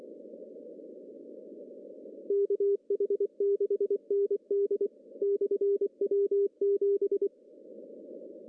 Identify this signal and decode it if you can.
Morse code, heard as a keyed tone, about 24 wpm, 400 Hz. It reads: KH6ND/W7
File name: CW / cqww-cw-k5tr